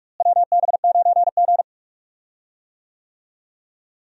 Morse code, heard as a keyed tone, about 42 wpm, 700 Hz. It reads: WB9G
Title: USA Call Signs - 4 Characters - Repeated 2x - 45wpm